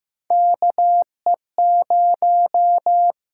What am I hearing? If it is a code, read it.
Morse code, heard as a keyed tone, about 15 wpm, 700 Hz. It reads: KE0
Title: USA Call Signs - 6 Characters - 8x Character Spacing - 15wpm